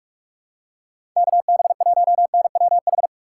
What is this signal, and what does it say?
Morse code, heard as a keyed tone, about 42 wpm, 700 Hz. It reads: KB1NWH
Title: USA Call Signs - 3x Character Spacing - 45wpm